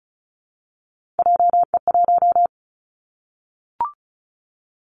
Morse code, heard as a keyed tone, about 34 wpm, 700 Hz. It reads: JE1
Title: Callsign Prefixes for Top 25 DXCC - Repeated 3x - 35wpm